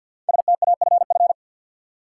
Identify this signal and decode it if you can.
Morse code, heard as a keyed tone, about 51 wpm, 700 Hz. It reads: STAFF
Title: Sentences from Top 1200 Words - New Word Drill - 50wpm